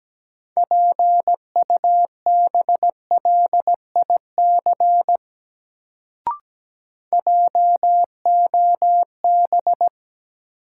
Morse code, heard as a keyed tone, about 17 wpm, 700 Hz. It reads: PUBLIC JOB
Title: Sentences from Top 600 Words - New Word Drill - 17wpm